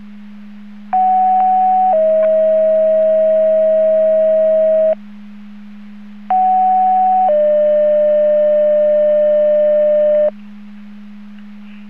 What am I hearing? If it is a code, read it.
Two-tone page, heard as a one-second tone, then a three-second tone, about 750 then 640 Hz, about 750 then 600 Hz. It is heard from 0.9 to 10.3 s.